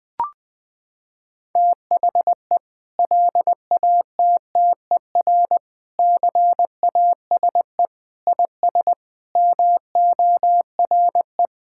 Morse code, heard as a keyed tone, about 20 wpm, 700 Hz. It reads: THE LATTER CASE IS MORE
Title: Sentences from Top 1600 Words 20wpm